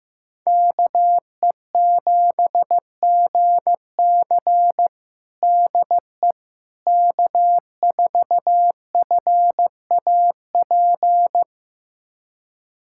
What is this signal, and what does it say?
Morse code, heard as a keyed tone, about 15 wpm, 700 Hz. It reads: KE7GC DE K4FAP
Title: USA Callsign DE Callsign 15wpm